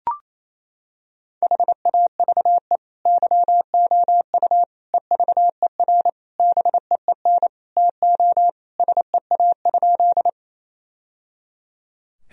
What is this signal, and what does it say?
Morse code, heard as a keyed tone, about 28 wpm, 700 Hz. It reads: HAVE YOU EVER BEEN TO SEA?
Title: Sentences from Top 1000 Words - Review Part 1 - 28wpm